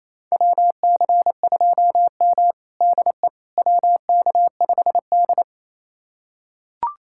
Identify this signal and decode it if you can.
Morse code, heard as a keyed tone, about 28 wpm, 700 Hz. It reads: WC2M DE WK5D
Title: USA Callsign DE Callsign - 4 Characters - 28wpm